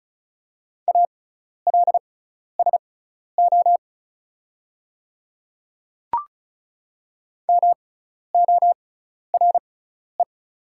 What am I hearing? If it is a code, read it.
Morse code, heard as a keyed tone, about 36 wpm, spaced as for 12 wpm, 700 Hz. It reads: ALSO MORE
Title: 6x Character Spacing - Top 200 Words - 35wpm